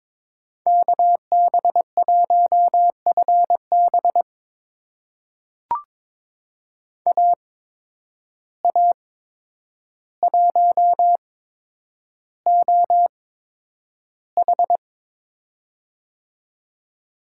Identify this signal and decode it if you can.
Morse code, heard as a keyed tone, about 22 wpm, spaced as for 9 wpm, 700 Hz. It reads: KB1FB AA1OH
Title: USA Call Signs - 5 Characters - 8x Character Spacing - 22wpm